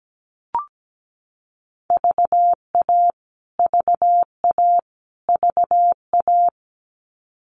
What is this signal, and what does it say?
Morse code, heard as a keyed tone, about 17 wpm, 700 Hz. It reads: VA VA VA
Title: US States Repeated 3x 17wpm